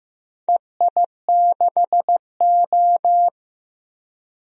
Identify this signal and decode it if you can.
Morse code, heard as a keyed tone, about 15 wpm, 700 Hz. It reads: EI6O
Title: DXCC-100 Call Signs - 4 Characters - 15wpm